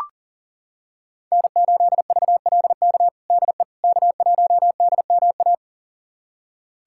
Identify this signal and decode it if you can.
Morse code, heard as a keyed tone, about 40 wpm, 700 Hz. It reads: N8VLK DE K1DMA
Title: USA Callsign DE Callsign - 5 Characters - 40wpm